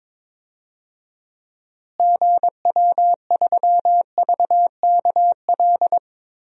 Morse code, heard as a keyed tone, about 22 wpm, 700 Hz. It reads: GW3VKL
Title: DXCC-100 Call Signs - 6 Characters - Repeated 3x - 22wpm